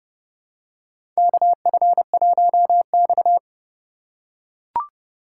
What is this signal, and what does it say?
Morse code, heard as a keyed tone, about 31 wpm, 700 Hz. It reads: KF1X
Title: USA Call Signs - 4 Characters - 3x Character Spacing - 30wpm